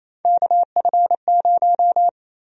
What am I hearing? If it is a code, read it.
Morse code, heard as a keyed tone, about 28 wpm, 700 Hz. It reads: KF0
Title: USA Call Sign Prefixes - Repeated 2x - 28wpm